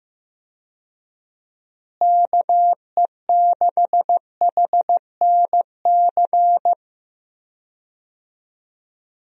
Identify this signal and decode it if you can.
Morse code, heard as a keyed tone, about 15 wpm, 700 Hz. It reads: KE6HNC
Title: USA Call Signs - 6 Characters - Repeated 2x - 15wpm